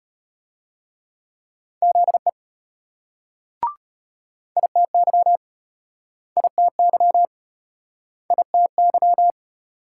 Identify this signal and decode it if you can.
Morse code, heard as a keyed tone, about 34 wpm, 700 Hz. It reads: ZE ITY ITY ITY